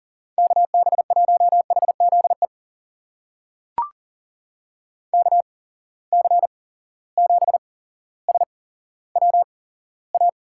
Morse code, heard as a keyed tone, about 40 wpm, spaced as for 17 wpm, 700 Hz. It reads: KB1HZE KC7SWA
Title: USA Call Signs - 6 Characters - 8x Character Spacing - 40wpm